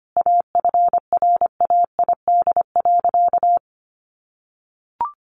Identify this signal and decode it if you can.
Morse code, heard as a keyed tone, about 25 wpm, 700 Hz. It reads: AFRAID.
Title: Sentences from Top 1500 Words 25wpm